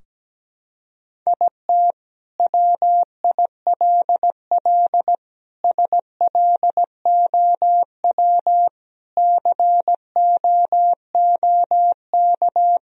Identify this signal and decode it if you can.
Morse code, heard as a keyed tone, about 17 wpm, 700 Hz. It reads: IT WILL SLOW COOK
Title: Sentences from Top 1000 Words - Review Part 4 - 17wpm